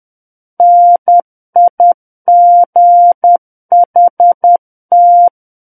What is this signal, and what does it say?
Morse code, heard as a keyed tone, about 10 wpm, 700 Hz. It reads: NIGHT
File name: Book One / 01-Chapter 1 - 10 WPM